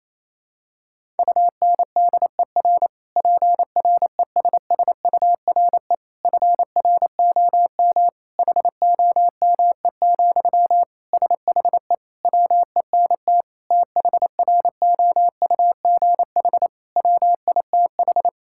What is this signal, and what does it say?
Morse code, heard as a keyed tone, about 28 wpm, 700 Hz. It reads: UNDER PRESSURE FROM HOME, SHE WENT THROUGH WITH
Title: Sentences from Top 1000 Words - Review Part 6 - 28wpm